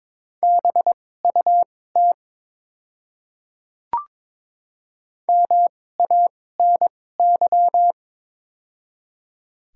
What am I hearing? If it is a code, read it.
Morse code, heard as a keyed tone, about 22 wpm, spaced as for 16 wpm, 700 Hz. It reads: BUT MANY
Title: 2x Character Spacing - Top 100 Words - 22wpm